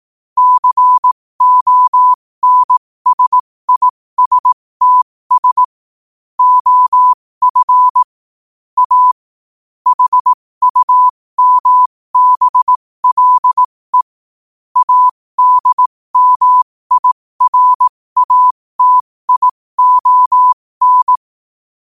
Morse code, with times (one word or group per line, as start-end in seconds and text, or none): 0.4-5.6 CONSISTS
6.4-8.0 OF
8.8-9.1 A
9.9-14.0 HUMBLE
14.8-21.2 ADMIRATION